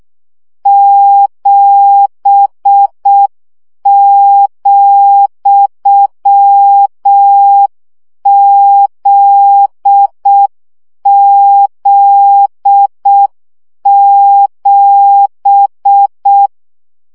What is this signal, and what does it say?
Morse code, heard as a keyed tone, about 6 wpm, 800 Hz. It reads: 7,ZZ7
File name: CW gyakorlas z 7 vesszo 5 baud